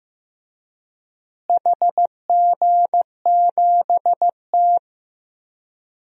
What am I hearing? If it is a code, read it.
Morse code, heard as a keyed tone, about 15 wpm, 700 Hz. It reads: HG7T
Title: DXCC-100 Call Signs - 4 Characters - Repeated 2x - 15wpm